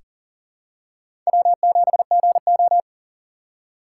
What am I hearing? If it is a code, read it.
Morse code, heard as a keyed tone, about 37 wpm, 700 Hz. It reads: W7GO
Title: USA Call Signs - 4 Characters - 4x Character Spacing - 40wpm